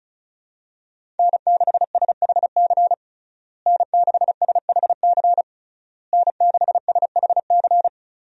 Morse code, heard as a keyed tone, about 35 wpm, 700 Hz. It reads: N6SHC N6SHC N6SHC
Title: USA Call Signs - 5 Characters - Repeated 3x - 35wpm